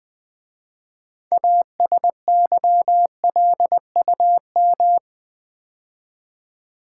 Morse code, heard as a keyed tone, about 20 wpm, 700 Hz. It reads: ASYLUM
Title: 6 Letter Words - First 3 Letters Given - 20wpm